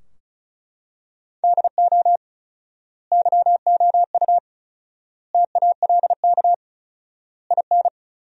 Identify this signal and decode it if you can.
Morse code, heard as a keyed tone, about 34 wpm, 700 Hz. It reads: DO YOU TALK IN
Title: 4x Word-Spacing - Sentences from Top 500 Words - 35wpm